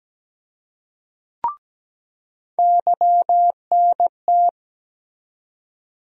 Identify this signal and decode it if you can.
Morse code, heard as a keyed tone, about 17 wpm, 700 Hz. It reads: YNT
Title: USA Call Sign Postfixes - 3 Letters 17wpm